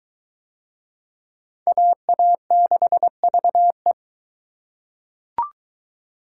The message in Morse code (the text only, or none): AA6VE